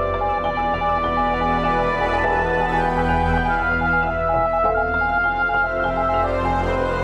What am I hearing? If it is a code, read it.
Morse code, heard as a keyed tone, about 20 wpm, 820 Hz. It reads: YOU.ABOUT